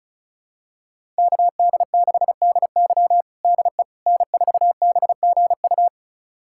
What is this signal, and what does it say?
Morse code, heard as a keyed tone, about 36 wpm, 700 Hz. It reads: KD6DY DE N4BGU
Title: USA Callsign DE Callsign - 5 Characters - 35wpm